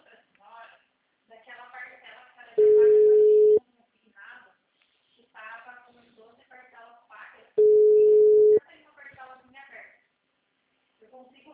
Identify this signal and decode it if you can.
Ringback tone (Europe, 425 Hz), with speech.